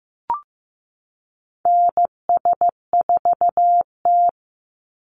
Morse code, heard as a keyed tone, about 15 wpm, 700 Hz. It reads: NS4T
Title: Call Signs Repeated 3x - 15wpm